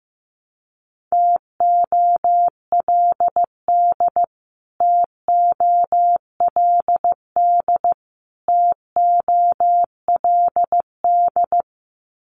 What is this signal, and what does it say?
Morse code, heard as a keyed tone, about 15 wpm, 700 Hz. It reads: TOLD TOLD TOLD
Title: Top 500 Words Repeated 15wpm